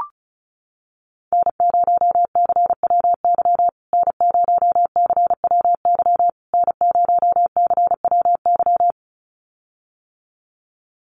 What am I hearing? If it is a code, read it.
Morse code, heard as a keyed tone, about 34 wpm, 700 Hz. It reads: N0CWY N0CWY N0CWY
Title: USA Call Signs Repeated 3x 35wpm